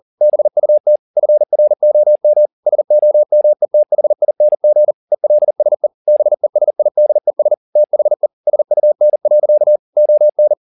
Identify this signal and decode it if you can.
Morse code, heard as a keyed tone, about 40 wpm, 600 Hz. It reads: BUT FROM SOMETHING ELSE BESIDES THE SUN. ON
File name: the old man and the sea 2 40wpm